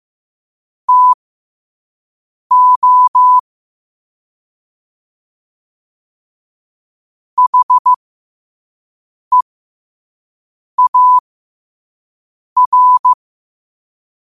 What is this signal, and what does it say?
Morse code, heard as a keyed tone, about 15 wpm, spaced as for 5 wpm, 1000 Hz. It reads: TO HEAR